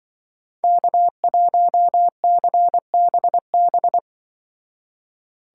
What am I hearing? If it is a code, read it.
Morse code, heard as a keyed tone, about 23 wpm, 700 Hz. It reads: K1CBB
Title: Speed Racing - USA 1x3 Call Signs 20wpm